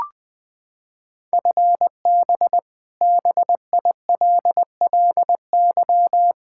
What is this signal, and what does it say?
Morse code, heard as a keyed tone, about 20 wpm, 700 Hz. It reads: FB BILLY
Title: K1USN SST QSO Practice 20wpm